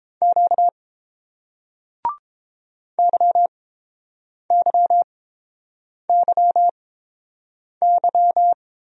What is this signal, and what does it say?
Morse code, heard as a keyed tone, about 30 wpm, spaced as for 7 wpm, 700 Hz. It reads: Q YYYY